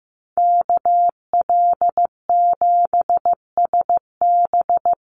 Morse code, heard as a keyed tone, about 15 wpm, 700 Hz. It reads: KL7SB
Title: Call Signs Repeated 2x - 15wpm